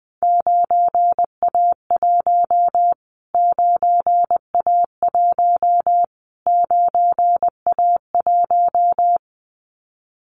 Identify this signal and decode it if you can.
Morse code, heard as a keyed tone, about 20 wpm, 700 Hz. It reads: 9A1 9A1 9A1
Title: Callsign Prefixes for Top 25 DXCC - Repeated 3x - 20wpm